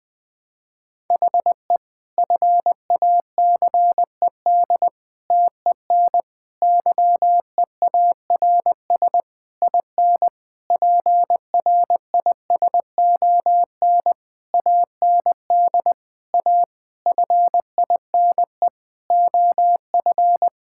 Morse code, heard as a keyed tone, about 20 wpm, 700 Hz. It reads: HE FACED TEN YEARS IN PRISON AND A FINE OF
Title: Sentences with 9-Word Phrases - 20wpm